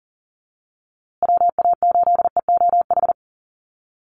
Morse code, heard as a keyed tone, about 40 wpm, 700 Hz. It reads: WA8EOH
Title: USA Call Signs Repeated 2x 40wpm